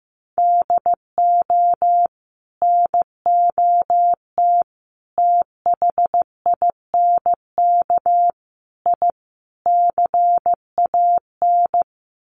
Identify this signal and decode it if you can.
Morse code, heard as a keyed tone, about 15 wpm, 700 Hz. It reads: DO NOT THINK I CAN
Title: Sets of 5 Words Encore2 15wpm